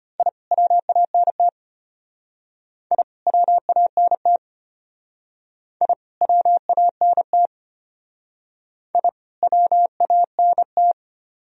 Morse code, heard as a keyed tone, about 32 wpm, 700 Hz. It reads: I WANT I WANT I WANT I WANT